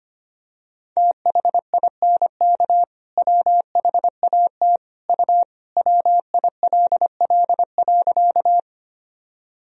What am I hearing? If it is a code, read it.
Morse code, heard as a keyed tone, about 25 wpm, 700 Hz. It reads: THINK WHAT U WILL.
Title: Sentences from Top 200 Words and Abbreviations 25wpm